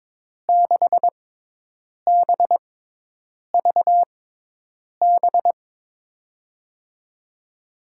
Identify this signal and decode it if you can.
Morse code, heard as a keyed tone, about 22 wpm, spaced as for 8 wpm, 700 Hz. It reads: 6BVB
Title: USA Call Signs - 6 Characters - 6x Character Spacing - 22wpm